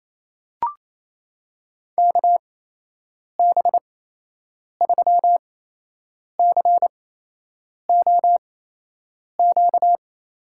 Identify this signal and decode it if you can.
Morse code, heard as a keyed tone, about 28 wpm, spaced as for 8 wpm, 700 Hz. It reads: KB3COQ
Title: USA Call Signs - 6 Characters - 8x Character Spacing - 28wpm